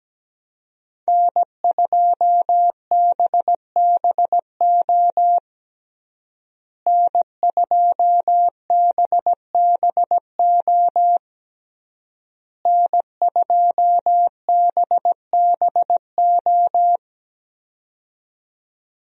Morse code, heard as a keyed tone, about 17 wpm, 700 Hz. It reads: N2BBO N2BBO N2BBO
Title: USA Call Signs - 5 Characters - Repeated 3x - 17wpm